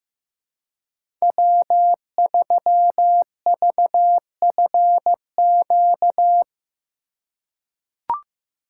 Morse code, heard as a keyed tone, about 15 wpm, 700 Hz. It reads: W3VFQ
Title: USA Call Signs - 5 Characters - 2x Character Spacing - 15wpm